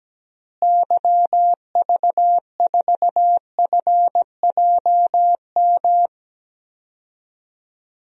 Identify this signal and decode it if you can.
Morse code, heard as a keyed tone, about 17 wpm, 700 Hz. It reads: YV4FJM
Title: DXCC-100 Call Signs - 6 Characters - Repeated 2x - 17wpm